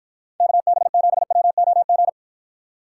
Morse code, heard as a keyed tone, about 52 wpm, 700 Hz. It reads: KB7WOG